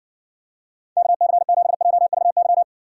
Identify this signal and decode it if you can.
Morse code, heard as a keyed tone, about 60 wpm, 700 Hz. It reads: KC7JVY